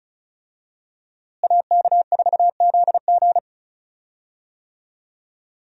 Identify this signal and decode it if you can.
Morse code, heard as a keyed tone, about 36 wpm, 700 Hz. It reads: AK4ZG
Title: USA Call Signs - 5 Characters - Repeated 2x - 35wpm